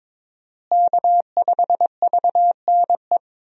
Morse code, heard as a keyed tone, about 22 wpm, 700 Hz. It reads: K5VNE
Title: USA Call Signs - 5 Characters - Repeated 2x - 22wpm